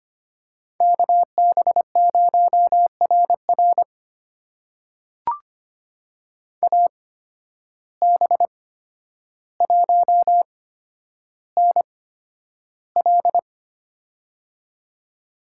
Morse code, heard as a keyed tone, about 25 wpm, spaced as for 11 wpm, 700 Hz. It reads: KB0RR AB1NL
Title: USA Call Signs - 5 Characters - 8x Character Spacing - 25wpm